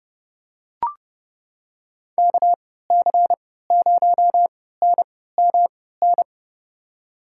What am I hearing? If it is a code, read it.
Morse code, heard as a keyed tone, about 31 wpm, spaced as for 17 wpm, 700 Hz. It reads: KC0NMN
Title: USA Call Signs - 3x Character Spacing - 30wpm